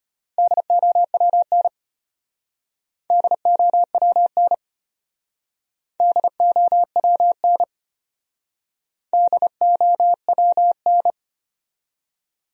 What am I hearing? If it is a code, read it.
Morse code, heard as a keyed tone, about 33 wpm, 700 Hz. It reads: DOWN DOWN DOWN DOWN